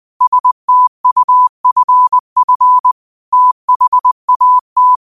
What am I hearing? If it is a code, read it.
Morse code, heard as a keyed tone, about 20 wpm, 1000 Hz. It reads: STUFF THAT